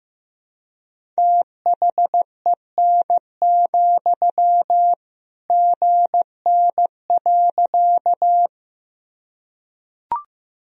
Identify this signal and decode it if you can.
Morse code, heard as a keyed tone, about 15 wpm, 700 Hz. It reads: THEN, GN.
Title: Sentences from Top 300 Words and Abbreviations 15wpm